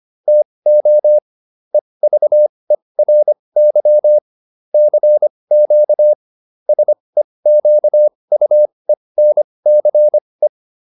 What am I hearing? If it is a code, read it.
Morse code, heard as a keyed tone, about 25 wpm, spaced as for 20 wpm, 600 Hz. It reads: TO EVERY CQ SEQUENCE